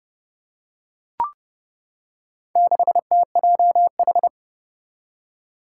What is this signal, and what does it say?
Morse code, heard as a keyed tone, about 30 wpm, 700 Hz. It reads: 6TJH
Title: Four Challenging Letter-Number Combinations 30wpm